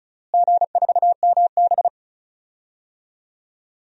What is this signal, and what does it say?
Morse code, heard as a keyed tone, about 34 wpm, 700 Hz. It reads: G4MB
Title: DXCC-100 Call Signs - 4 Characters - Repeated 2x - 35wpm